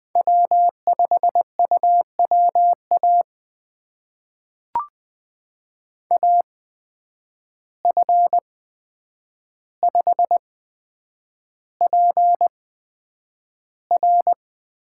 Morse code, heard as a keyed tone, about 20 wpm, spaced as for 9 wpm, 700 Hz. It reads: W5UWA AF5PR
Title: USA Call Signs - 5 Characters - 8x Character Spacing - 20wpm